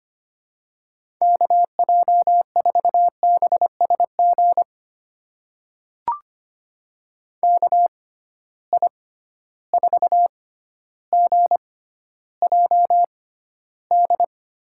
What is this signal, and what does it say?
Morse code, heard as a keyed tone, about 25 wpm, spaced as for 13 wpm, 700 Hz. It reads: KJ4BSG KI4GJD
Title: USA Call Signs - 6 Characters - 6x Character Spacing - 25wpm